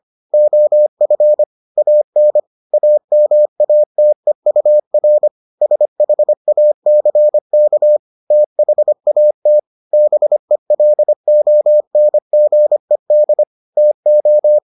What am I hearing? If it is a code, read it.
Morse code, heard as a keyed tone, about 25 wpm, 600 Hz. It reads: OF AN AMATEUR SHACK THAT BELONGED TO